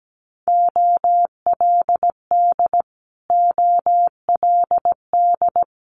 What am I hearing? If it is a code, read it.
Morse code, heard as a keyed tone, about 17 wpm, 700 Hz. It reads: OLD OLD
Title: Multi-letter Phonemes Repeated 3x 17wpm